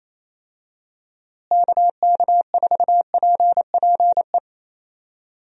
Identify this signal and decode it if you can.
Morse code, heard as a keyed tone, about 28 wpm, 700 Hz. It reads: KK4PPE
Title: USA Call Signs - 6 Characters - Repeated 3x - 28wpm